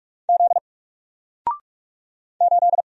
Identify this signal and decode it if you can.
Morse code, heard as a keyed tone, about 44 wpm, 700 Hz. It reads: Z 8
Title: Single Letter-Number Alt 45wpm